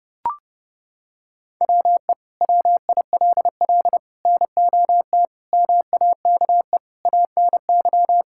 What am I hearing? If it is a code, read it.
Morse code, heard as a keyed tone, about 32 wpm, 700 Hz. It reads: WE WILL NOT MAKE ANY
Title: Sentences from Top 1000 Words - Review Part 5 - 30wpm